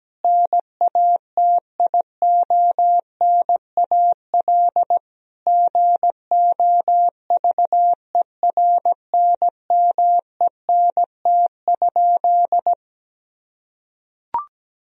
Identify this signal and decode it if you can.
Morse code, heard as a keyed tone, about 17 wpm, 700 Hz. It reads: NATIONAL GOVERNMENT?
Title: Sentences from Top 1000 Words - Review Part 4 - 17wpm